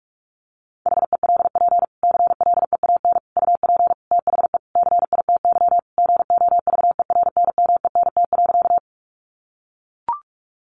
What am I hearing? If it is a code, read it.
Morse code, heard as a keyed tone, about 44 wpm, 700 Hz. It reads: HELP CLEAN UP THE CITY GOVERNMENT.
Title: Sentences from Top 1100 Words 45wpm